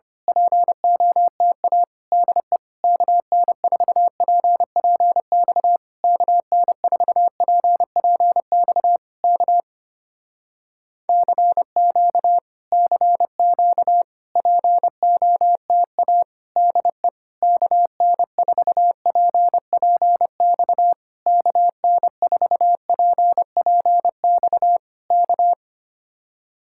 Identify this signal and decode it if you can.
Morse code, heard as a keyed tone, about 25 wpm, 700 Hz. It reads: POTA DE KN4PPX KN4PPX K CQ CQ POTA DE KN4PPX KN4PPX K
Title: Speed-Racing Common QSO Phrases - 25wpm